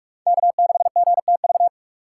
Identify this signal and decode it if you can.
Morse code, heard as a keyed tone, about 40 wpm, 700 Hz. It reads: K6GTV